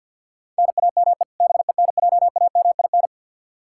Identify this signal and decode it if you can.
Morse code, heard as a keyed tone, about 51 wpm, 700 Hz. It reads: NAME BENJAMIN
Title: K1USN SST QSO Practice 50wpm